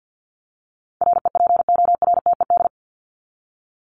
Morse code, heard as a keyed tone, about 50 wpm, 700 Hz. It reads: REPORTED
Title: ED Suffix - Top 100 Words with Base Words 50wpm